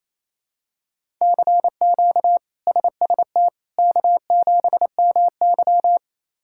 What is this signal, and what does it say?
Morse code, heard as a keyed tone, about 27 wpm, 700 Hz. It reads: CQ SST K7MY
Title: K1USN SST QSO Practice - 28wpm